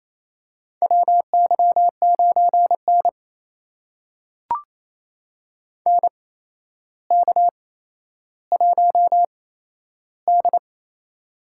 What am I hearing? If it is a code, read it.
Morse code, heard as a keyed tone, about 28 wpm, spaced as for 12 wpm, 700 Hz. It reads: WY9N NK1D
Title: USA Call Signs - 4 Characters - 8x Character Spacing - 28wpm